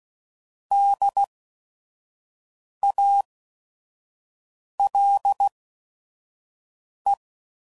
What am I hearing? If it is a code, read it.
Morse code, heard as a keyed tone, about 16 wpm, spaced as for 5 wpm, 800 Hz. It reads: DALE
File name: Prefixes / By First Letter / Ds